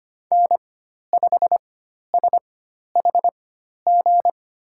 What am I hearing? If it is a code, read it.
Morse code, heard as a keyed tone, about 25 wpm, spaced as for 12 wpm, 700 Hz. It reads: N5SHG
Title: USA Call Signs - 5 Characters - 4x Character Spacing - 25wpm